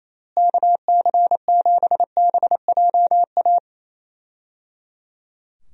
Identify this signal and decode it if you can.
Morse code, heard as a keyed tone, about 28 wpm, 700 Hz. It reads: KC7BJA
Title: USA Call Signs - 6 Characters - Repeated 3x - 28wpm